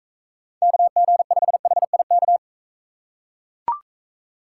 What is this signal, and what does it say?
Morse code, heard as a keyed tone, about 42 wpm, 700 Hz. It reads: KG5HIK